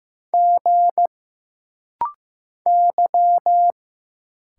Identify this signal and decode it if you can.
Morse code, heard as a keyed tone, about 15 wpm, 700 Hz. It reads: G Y
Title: Single Letters - Rapid-Fire 15wpm